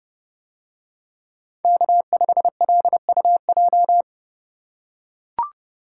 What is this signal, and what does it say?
Morse code, heard as a keyed tone, about 30 wpm, 700 Hz. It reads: K5LUJ